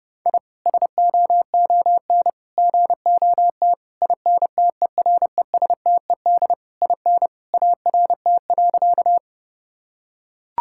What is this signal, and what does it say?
Morse code, heard as a keyed tone, about 30 wpm, 700 Hz. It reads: I SOON GOT INTERESTED IN ART.
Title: Sentences from Top 1000 Words - Review Part 6 - 30wpm